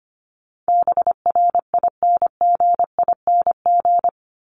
Morse code, heard as a keyed tone, about 25 wpm, 700 Hz. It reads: BRINGING
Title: ING Suffix - Top 100 Words with Base Words 25wpm